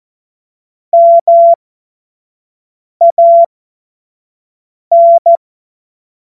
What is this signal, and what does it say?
Morse code, heard as a keyed tone, about 14 wpm, spaced as for 5 wpm, 680 Hz. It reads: MAN
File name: Morsekurs / Witze / Witz 31. Januar 2023 25 F70